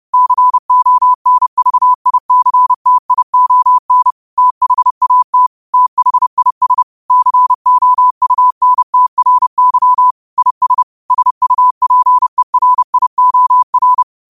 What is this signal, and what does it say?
Morse code, heard as a keyed tone, about 30 wpm, 1000 Hz. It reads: CONVICTION THAT THIS COUNTRY IS SUPERIOR
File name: qotdfiles / 30wpm